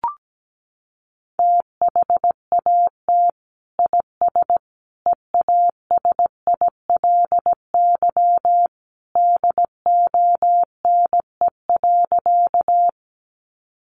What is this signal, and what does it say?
Morse code, heard as a keyed tone, about 17 wpm, 700 Hz. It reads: THAT IS EASILY DONE.
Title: Sentences from Top 900 Words 17wpm